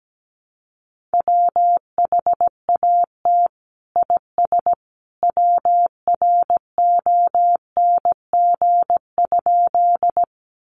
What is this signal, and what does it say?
Morse code, heard as a keyed tone, about 17 wpm, 700 Hz. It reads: WHAT IS WRONG?
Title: Sentences from Top 900 Words 17wpm